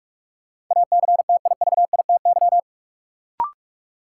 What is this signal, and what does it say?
Morse code, heard as a keyed tone, about 42 wpm, 700 Hz. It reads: ACTIVITY
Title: Sentences from Top 1000 Words - Review Part 4 - New Word Drill - 45wpm